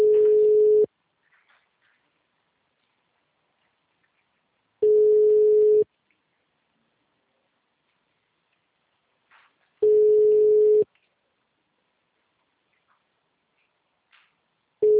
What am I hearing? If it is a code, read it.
Ringback tone (Europe, 425 Hz).